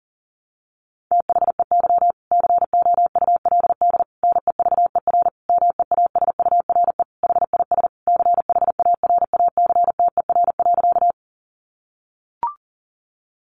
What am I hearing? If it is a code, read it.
Morse code, heard as a keyed tone, about 40 wpm, 700 Hz. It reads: THEY COULD NEVER MEASURE HIS CHARACTER.
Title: Sentences from Top 1200 Words 40wpm